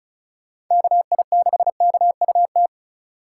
Morse code, heard as a keyed tone, about 36 wpm, 700 Hz. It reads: KI6KUT
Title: USA Call Signs - 3x Character Spacing - 35wpm